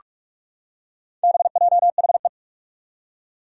Morse code, heard as a keyed tone, about 44 wpm, 700 Hz. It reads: BJHE
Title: Four Challenging Letter-Number Combinations 45wpm